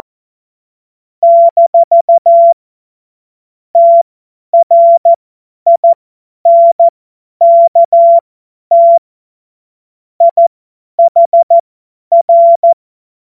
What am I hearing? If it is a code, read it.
Morse code, heard as a keyed tone, about 14 wpm, spaced as for 10 wpm, 680 Hz. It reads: - TRINKT IHR
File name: Morsekurs / Witze / Witz 01. November 2022 50 F70